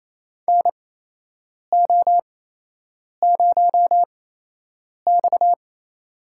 Morse code, heard as a keyed tone, about 28 wpm, spaced as for 8 wpm, 700 Hz. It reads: NO0X
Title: USA Call Signs - 4 Characters - 8x Character Spacing - 28wpm